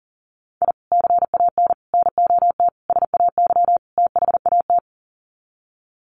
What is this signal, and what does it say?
Morse code, heard as a keyed tone, about 40 wpm, 700 Hz. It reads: I CAN NOT SAY THAT
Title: Sets of 5 Words Encore2 40wpm